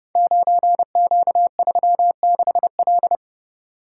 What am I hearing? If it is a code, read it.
Morse code, heard as a keyed tone, about 30 wpm, 700 Hz. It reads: 9Q36L
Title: Code Groups - 5 Letter:Numbers - 30wpm